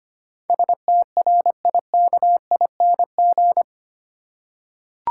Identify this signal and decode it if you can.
Morse code, heard as a keyed tone, about 25 wpm, 700 Hz. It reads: STRIKING
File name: 8 Letter Words - First 3 Letters Given - 25wpm